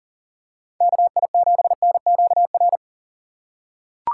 Morse code, heard as a keyed tone, about 40 wpm, 700 Hz. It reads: KI7NQR
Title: USA Call Signs - 6 Characters - 6x Character Spacing - 40wpm